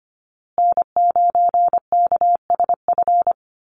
Morse code, heard as a keyed tone, about 25 wpm, 700 Hz. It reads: N9KSF
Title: USA Call Signs Repeated 2x 25wpm